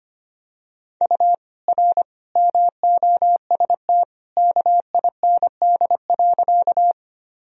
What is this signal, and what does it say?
Morse code, heard as a keyed tone, about 25 wpm, 700 Hz. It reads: U R MOST KIND.
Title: Sentences from Top 200 Words and Abbreviations 25wpm